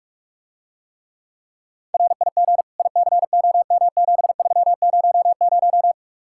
Morse code, heard as a keyed tone, about 49 wpm, 700 Hz. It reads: RIG ICOM7300